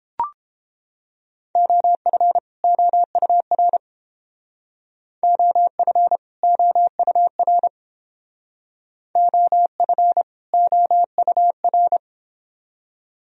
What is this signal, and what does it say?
Morse code, heard as a keyed tone, about 29 wpm, 700 Hz. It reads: OF OUR OF OUR OF OUR